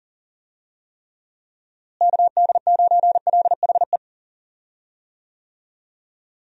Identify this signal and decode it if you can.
Morse code, heard as a keyed tone, about 38 wpm, 700 Hz. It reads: KD9LHE
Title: USA Call Signs - 6 Characters - Repeated 2x - 40wpm